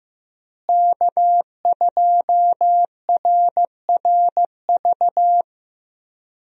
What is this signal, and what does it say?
Morse code, heard as a keyed tone, about 16 wpm, 700 Hz. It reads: K2RRV
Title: USA Call Signs - 5 Characters - Repeated 3x - 15wpm